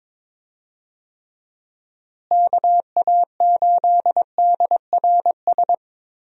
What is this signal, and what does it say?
Morse code, heard as a keyed tone, about 22 wpm, 700 Hz. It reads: KA8DRS
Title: USA Call Signs - 6 Characters - Repeated 2x - 22wpm